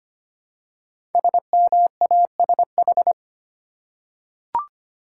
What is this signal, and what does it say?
Morse code, heard as a keyed tone, about 25 wpm, 700 Hz. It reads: SMASH
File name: 5 Letter Words - First 3 Letters Given - 25wpm